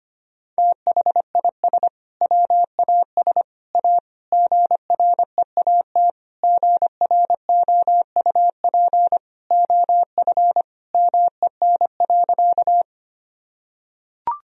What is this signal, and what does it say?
Morse code, heard as a keyed tone, about 25 wpm, 700 Hz. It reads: THIS WAS A GREAT GROUP OF MEN.
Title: Sentences from Top 1000 Words - Review Part 1 - 25wpm